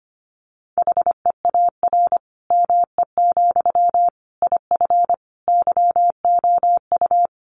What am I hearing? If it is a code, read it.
Morse code, heard as a keyed tone, about 25 wpm, 700 Hz. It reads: HEAR ME, IF YOU
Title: Sentences from Top 1500 Words 25wpm